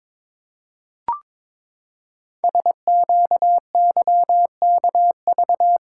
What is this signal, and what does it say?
Morse code, heard as a keyed tone, about 22 wpm, 700 Hz. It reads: SQYKV
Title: Code Groups - 5 Letters - 22wpm